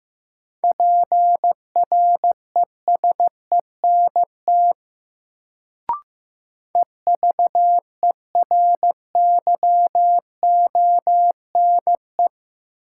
Morse code, heard as a keyed tone, about 15 wpm, 700 Hz. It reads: PRESENT EVERYONE
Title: Sentences from Top 800 Words - New Word Drill - 15wpm